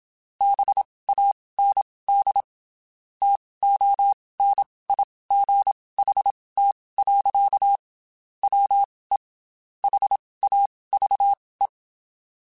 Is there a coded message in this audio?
Yes: Morse code from 0.4 to 11.7 s, heard as a keyed tone, about 27 wpm, spaced as for 19 wpm, 800 Hz. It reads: BAND TONIGHT. WE HAVE